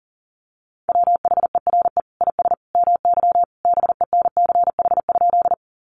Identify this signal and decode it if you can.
Morse code, heard as a keyed tone, about 37 wpm, 700 Hz. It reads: WHERE IS MY BENCH?
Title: Sentences from Top 1400 Words 40wpm